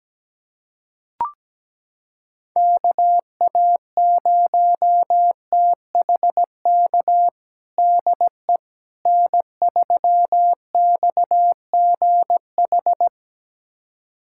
Morse code, heard as a keyed tone, about 17 wpm, 700 Hz. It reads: KA0THK DE N3XGH
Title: USA Callsign DE Callsign 17wpm